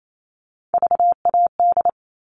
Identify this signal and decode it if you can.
Morse code, heard as a keyed tone, about 28 wpm, 700 Hz. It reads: VAD
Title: All Cut Numbers - 3-Digit 28wpm